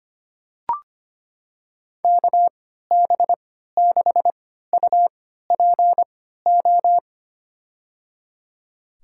Morse code, heard as a keyed tone, about 25 wpm, spaced as for 14 wpm, 700 Hz. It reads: KB6UPO
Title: USA Call Signs - 6 Characters - 3x Character Spacing - 25wpm